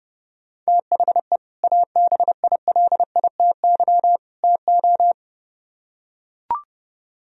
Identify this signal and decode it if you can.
Morse code, heard as a keyed tone, about 30 wpm, 700 Hz. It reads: THE ABILITY TO